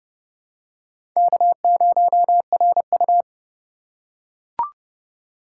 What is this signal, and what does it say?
Morse code, heard as a keyed tone, about 32 wpm, 700 Hz. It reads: K0RU
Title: USA Call Signs - 4 Characters - 4x Character Spacing - 30wpm